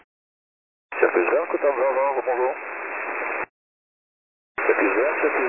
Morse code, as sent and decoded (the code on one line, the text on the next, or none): none